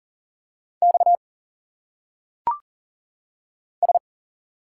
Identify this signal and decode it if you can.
Morse code, heard as a keyed tone, about 40 wpm, 700 Hz. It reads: X S
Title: Single Letters - Phonetic Pronunciation 40wpm